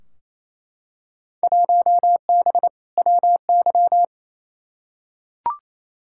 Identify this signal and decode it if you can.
Morse code, heard as a keyed tone, about 28 wpm, 700 Hz. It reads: 1B WY
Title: Field Day Exchanges Encore - 28wpm